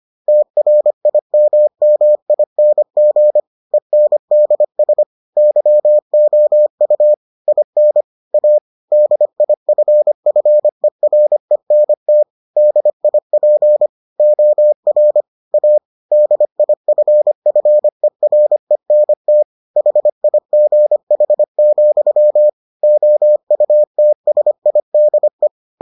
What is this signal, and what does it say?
Morse code, heard as a keyed tone, about 25 wpm, 600 Hz. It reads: TRIMMING ENDS YOU IN A DIFFERENT DIP OR A DIFFERENT HIGH, OUTSIDE